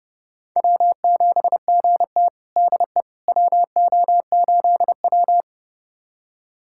Morse code, heard as a keyed tone, about 31 wpm, 700 Hz. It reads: W7GT DE WO8W
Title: USA Callsign DE Callsign - 4 Characters - 30wpm